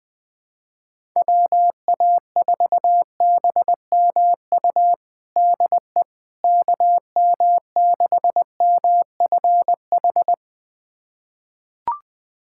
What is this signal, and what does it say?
Morse code, heard as a keyed tone, about 20 wpm, 700 Hz. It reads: WA4BMU DE KM6MFH
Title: USA Callsign DE Callsign 20wpm